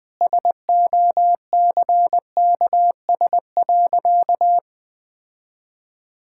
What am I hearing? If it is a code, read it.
Morse code, heard as a keyed tone, about 20 wpm, 700 Hz. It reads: SOCKS.
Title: Easy Sentences with Hints - 20wpm